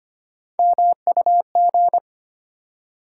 Morse code, heard as a keyed tone, about 25 wpm, 700 Hz. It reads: MUG
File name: practice / 25wpm / Difficult Words with Hints - 25wpm